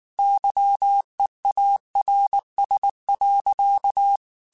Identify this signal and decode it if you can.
Morse code, heard as a keyed tone, about 19 wpm, 800 Hz. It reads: YEARS.
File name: QSOs / QSOs-mp3s / WA7QOX 2@18